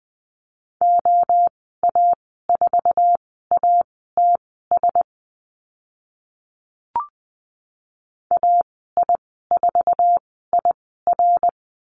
Morse code, heard as a keyed tone, about 20 wpm, spaced as for 14 wpm, 700 Hz. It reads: OA4ATS AI4IR
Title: 055 Lesson 10g - Letter r - Call Signs 20wpm